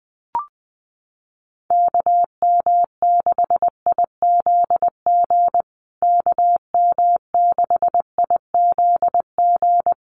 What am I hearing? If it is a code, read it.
Morse code, heard as a keyed tone, about 20 wpm, 700 Hz. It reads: KM6IZG KM6IZG
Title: USA Call Signs Repeated 3x 20wpm